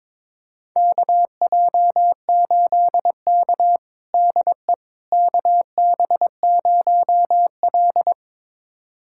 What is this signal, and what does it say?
Morse code, heard as a keyed tone, about 22 wpm, 700 Hz. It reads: KJ8K DE KB0L
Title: USA Callsign DE Callsign - 4 Characters - 22wpm